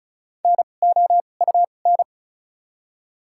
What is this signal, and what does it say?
Morse code, heard as a keyed tone, about 34 wpm, 700 Hz. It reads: N O U N
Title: 2x Character Spacing - Top 500 Words - 35wpm